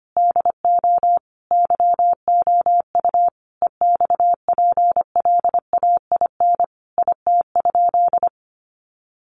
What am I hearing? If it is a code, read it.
Morse code, heard as a keyed tone, about 25 wpm, 700 Hz. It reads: DO YOU EXPLAIN IT?
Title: Sentences from Top 1100 Words 25wpm